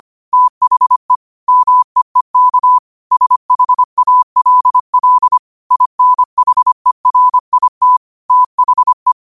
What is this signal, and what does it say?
Morse code, heard as a keyed tone, about 25 wpm, 1000 Hz. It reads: THE MEEK SHALL INHERIT THE